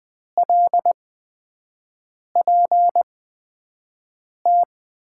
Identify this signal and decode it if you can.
Morse code, heard as a keyed tone, about 20 wpm, spaced as for 5 wpm, 700 Hz. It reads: LPT
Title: USA Call Signs - 8x Character Spacing - 20wpm